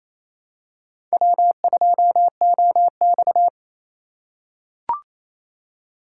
Morse code, heard as a keyed tone, about 28 wpm, 700 Hz. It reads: W2OX
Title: USA Call Signs - 4 Characters - 4x Character Spacing - 28wpm